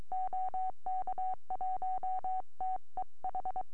Morse code, heard as a keyed tone, about 22 wpm, 750 Hz. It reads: OK1TEH
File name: ct3hf 4m